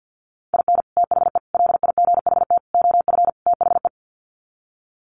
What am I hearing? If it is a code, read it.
Morse code, heard as a keyed tone, about 48 wpm, 700 Hz. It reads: IN THE LIGHT OF THE
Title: Sets of 5 Words Encore2 50wpm